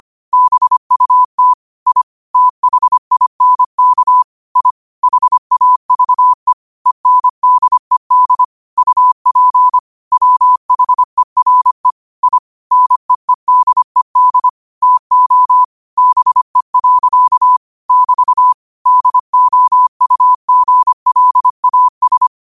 Morse code, heard as a keyed tone, about 25 wpm, 1000 Hz. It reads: BUT I THINK I HAVE ENDED UP WHERE I NEEDED TO BE. <BT> DOUGLAS